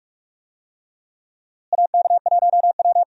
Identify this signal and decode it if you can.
Morse code, heard as a keyed tone, about 46 wpm, 700 Hz. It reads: AK1W
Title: USA Call Signs - 4 Characters - Repeated 3x - 45wpm